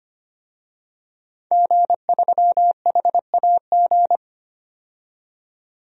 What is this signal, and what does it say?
Morse code, heard as a keyed tone, about 25 wpm, 700 Hz. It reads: G3HAG
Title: DXCC-100 Call Signs - 5 Characters - 25wpm